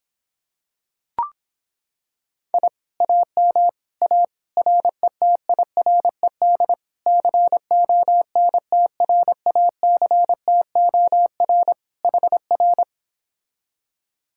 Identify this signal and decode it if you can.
Morse code, heard as a keyed tone, about 26 wpm, 700 Hz. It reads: I AM A RETIRED CONTRACTOR HR